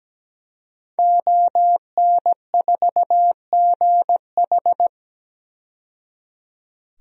Morse code, heard as a keyed tone, about 17 wpm, 700 Hz. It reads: ON4GH
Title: DXCC-100 Call Signs - 5 Characters - 17wpm